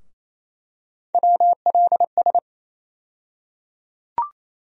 Morse code, heard as a keyed tone, about 27 wpm, 700 Hz. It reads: WLS
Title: USA Call Sign Postfixes - 3 Letters 28wpm